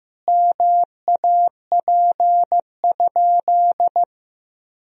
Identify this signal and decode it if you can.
Morse code, heard as a keyed tone, about 15 wpm, 700 Hz. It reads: MAP?
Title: Sentences from Top 1000 Words - Review Part 2 - 15wpm